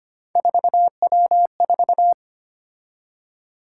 Morse code, heard as a keyed tone, about 25 wpm, 700 Hz. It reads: 4W4
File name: Three Letter-Number Combinations - 25wpm